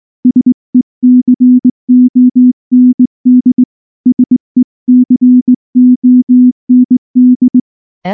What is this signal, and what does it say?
Morse code, heard as a keyed tone, about 19 wpm, 250 Hz. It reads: SECOND SECOND
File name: morse / morse-training-20WPM-4